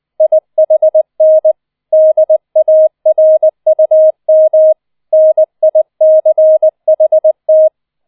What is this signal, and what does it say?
Morse code, heard as a keyed tone, about 20 wpm, 630 Hz. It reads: IHN DARUM NICHT
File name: 004 Vierter Streich 20wpm